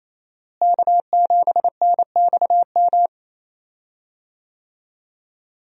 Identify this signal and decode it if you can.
Morse code, heard as a keyed tone, about 27 wpm, 700 Hz. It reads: K7NXM
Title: USA Call Signs - 5 Characters - Repeated 2x - 28wpm